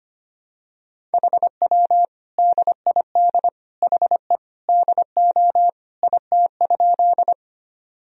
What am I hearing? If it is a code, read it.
Morse code, heard as a keyed tone, about 25 wpm, 700 Hz. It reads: HW DID HE DO IT?
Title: Sentences from Top 100 Words and Abbreviations 25wpm